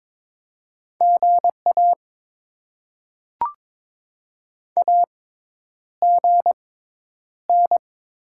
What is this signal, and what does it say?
Morse code, heard as a keyed tone, about 23 wpm, spaced as for 10 wpm, 690 Hz. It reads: GA AGN
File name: Top 20 QSO Elements - 6x Character Spacing - 22wpm